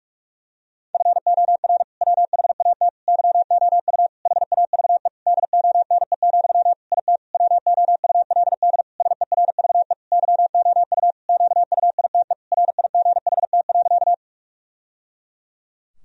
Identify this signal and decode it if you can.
Morse code, heard as a keyed tone, about 46 wpm, 700 Hz. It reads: FOR WHAT YOU HAVE DONE, IT WOULD SERVE YOU QUITE RIGHT.
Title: Sentences from Top 1000 Words - Review Part 2 - 45wpm